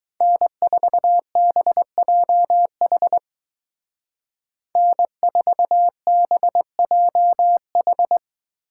Morse code, heard as a keyed tone, about 20 wpm, 700 Hz. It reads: N4BJH N4BJH
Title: Speed Racing - USA 1x3 Call Signs 17wpm